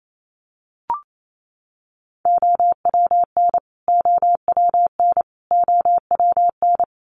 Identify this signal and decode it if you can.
Morse code, heard as a keyed tone, about 28 wpm, 700 Hz. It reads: OWN OWN OWN
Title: Top 500 Words Repeated 28wpm